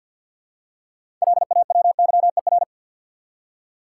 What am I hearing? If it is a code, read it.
Morse code, heard as a keyed tone, about 52 wpm, 700 Hz. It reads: LAWYER
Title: Six Letter Words - 50wpm